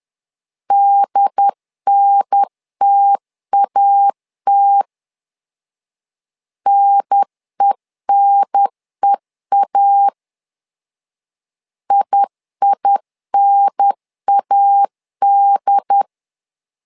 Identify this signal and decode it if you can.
Morse code, heard as a keyed tone, about 10 wpm, 800 Hz. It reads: DNTAT NENEA IINAD
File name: LEZIONI / 2 UHDN / 2 UHDN 10wpm 1904